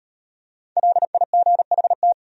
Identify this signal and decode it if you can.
Morse code, heard as a keyed tone, about 35 wpm, 700 Hz. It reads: LIGHT